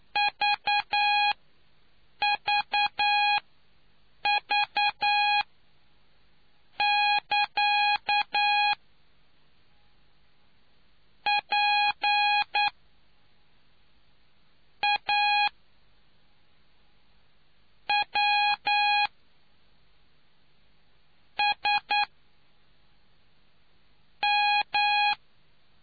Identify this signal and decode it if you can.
Morse code, heard as a keyed tone, about 9 wpm, spaced as for 4 wpm, 830 Hz. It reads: VVV<KA>PAWSM